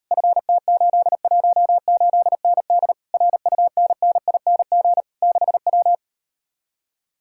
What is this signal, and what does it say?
Morse code, heard as a keyed tone, about 38 wpm, 700 Hz. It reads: FT818ND RUNNING 6W